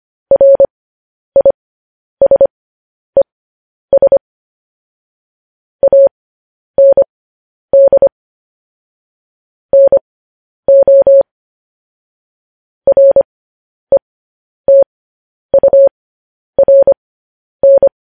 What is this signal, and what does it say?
Morse code, heard as a keyed tone, about 25 wpm, spaced as for 10 wpm, 550 Hz. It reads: RISES AND NO RETURN
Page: news-headlines-in-morse-code-at-10-wpm